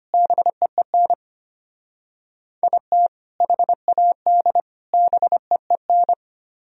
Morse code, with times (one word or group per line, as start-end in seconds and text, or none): 0.1-1.1 BEEN
2.6-3.1 IT
3.4-4.6 HAD
4.9-6.1 BEEN